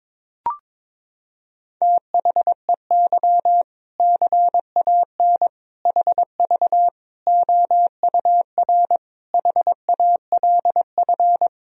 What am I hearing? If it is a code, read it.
Morse code, heard as a keyed tone, about 22 wpm, 700 Hz. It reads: THEY CAN HV OUR HALF
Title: Sentences from Top 400 Words and Abbreviations 22wpm